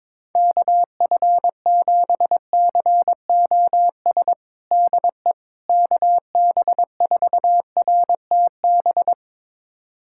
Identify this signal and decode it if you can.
Morse code, heard as a keyed tone, about 22 wpm, 700 Hz. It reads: KF7COS DE KB4RTB
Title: USA Callsign DE Callsign - 6 Characters - 22wpm